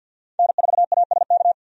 Morse code, heard as a keyed tone, about 49 wpm, 700 Hz. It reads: N4ASK
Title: USA Call Signs - 5 Characters - 50wpm